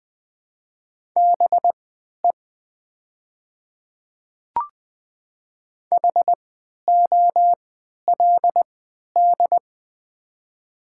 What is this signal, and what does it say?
Morse code, heard as a keyed tone, about 20 wpm, spaced as for 11 wpm, 700 Hz. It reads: BE HOLD
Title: 3x Character Spacing - Top 500 Words - 20wpm